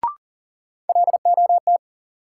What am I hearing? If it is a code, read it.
Morse code, heard as a keyed tone, about 40 wpm, 700 Hz. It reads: LOT
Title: Sets of 2 Words - New Word Drill - 40wpm